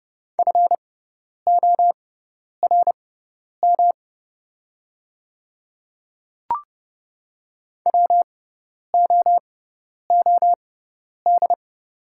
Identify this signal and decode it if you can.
Morse code, heard as a keyed tone, about 30 wpm, spaced as for 10 wpm, 700 Hz. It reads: FORM WOOD
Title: 6x Character Spacing - Top 500 Words - 30wpm